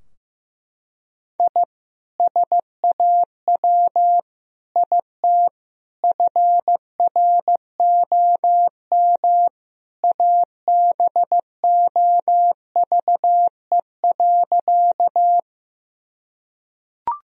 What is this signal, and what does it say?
Morse code, heard as a keyed tone, about 16 wpm, 700 Hz. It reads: I SAW IT FROM ABOVE.
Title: Sentences from Top 1000 Words - Review Part 2 - 15wpm